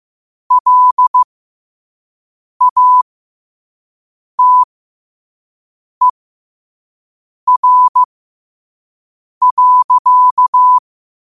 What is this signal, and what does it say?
Morse code, heard as a keyed tone, about 15 wpm, spaced as for 5 wpm, 1000 Hz. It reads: LATER.
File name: qotdfiles / 5wpm